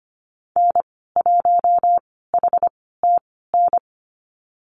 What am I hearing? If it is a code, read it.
Morse code, heard as a keyed tone, about 25 wpm, spaced as for 16 wpm, 690 Hz. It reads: N1HTN
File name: Lesson 15g - Number 5 - Call Signs 25wpm